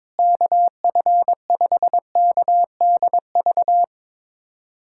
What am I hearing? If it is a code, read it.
Morse code, heard as a keyed tone, about 21 wpm, 700 Hz. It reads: KF5KDV
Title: USA Callsign DE Callsign 22wpm